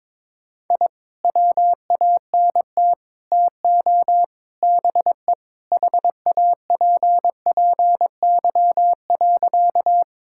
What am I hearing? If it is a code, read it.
Morse code, heard as a keyed tone, about 21 wpm, 700 Hz. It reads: I WANT TO BE HAPPY.
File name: practice / 22wpm / Sentences with 4-Word Phrases - 22wpm